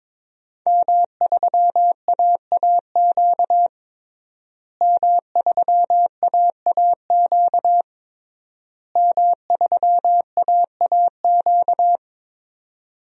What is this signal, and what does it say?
Morse code, heard as a keyed tone, about 22 wpm, 700 Hz. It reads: M3AAQ M3AAQ M3AAQ
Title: DXCC-100 Call Signs - 5 Characters - Repeated 3x - 22wpm